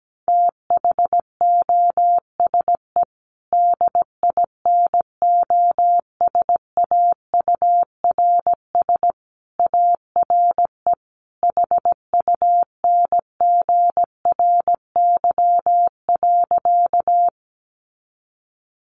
Morse code, heard as a keyed tone, about 17 wpm, 700 Hz. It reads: THOSE DINOSAURS ARE HUNGRY.
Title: A Sound Of Thunder 17wpm